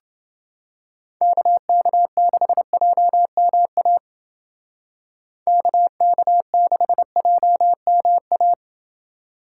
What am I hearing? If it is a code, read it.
Morse code, heard as a keyed tone, about 29 wpm, 700 Hz. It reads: KK6JMA KK6JMA